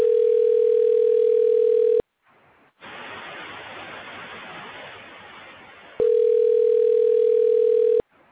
Ringback tone (North America), 440 + 480 Hz.
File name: recordings / RE1ce329f052e0f23e689cf3f512f722dc